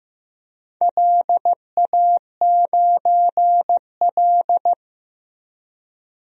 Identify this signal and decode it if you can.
Morse code, heard as a keyed tone, about 15 wpm, 700 Hz. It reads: LA9L
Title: DXCC-100 Call Signs - 4 Characters - Repeated 2x - 15wpm